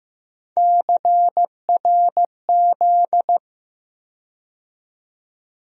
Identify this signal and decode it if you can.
Morse code, heard as a keyed tone, about 15 wpm, 700 Hz. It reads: CRZ
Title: USA Call Sign Postfixes - 3 Letters 15wpm